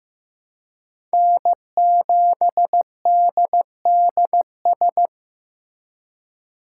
Morse code, heard as a keyed tone, about 18 wpm, 700 Hz. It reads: N7DDS
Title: Speed Racing - USA Call Signs 15wpm